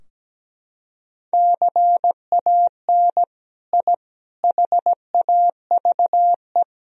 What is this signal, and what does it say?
Morse code, heard as a keyed tone, about 17 wpm, 700 Hz. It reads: CAN I HAVE
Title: Sentences from Top 1000 Words - Review Part 2 - 17wpm